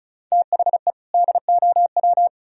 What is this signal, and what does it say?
Morse code, heard as a keyed tone, about 46 wpm, 700 Hz. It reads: THE DOW